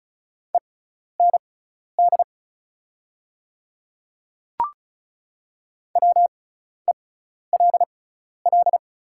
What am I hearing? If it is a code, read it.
Morse code, heard as a keyed tone, about 34 wpm, spaced as for 12 wpm, 700 Hz. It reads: END WELL
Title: 6x Character Spacing - Top 200 Words - 35wpm